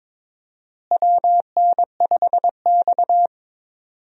Morse code, heard as a keyed tone, about 22 wpm, 700 Hz. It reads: WN5X
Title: USA Call Signs - 4 Characters - Repeated 3x - 22wpm